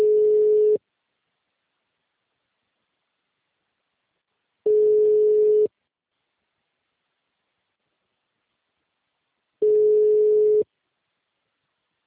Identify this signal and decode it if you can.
Ringback tone (Europe, 425 Hz).
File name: out- -2031- - - .28712